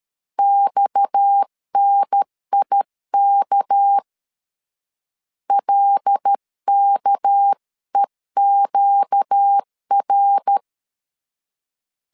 Morse code, heard as a keyed tone, about 13 wpm, 800 Hz. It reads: XNIK LKEQR
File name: CW / DET / 1 6 1500C S C 12wpm 3533